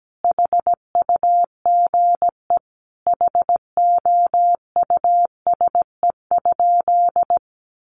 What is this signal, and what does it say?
Morse code, heard as a keyed tone, about 17 wpm, 700 Hz. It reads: HUGE HOUSE?
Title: Sentences from Top 1100 Words 17wpm